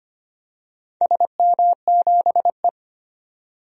Morse code, heard as a keyed tone, about 25 wpm, 700 Hz. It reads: SM7E
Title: DXCC-100 Call Signs - 4 Characters - 25wpm